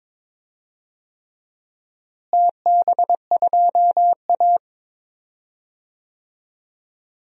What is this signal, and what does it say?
Morse code, heard as a keyed tone, about 21 wpm, 700 Hz. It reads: TB2A
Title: DXCC-100 Call Signs - 4 Characters - Repeated 3x - 22wpm